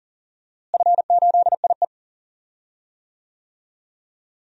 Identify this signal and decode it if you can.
Morse code, heard as a keyed tone, about 37 wpm, 700 Hz. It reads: F8IE
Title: DXCC-100 Call Signs - 4 Characters - Repeated 2x - 40wpm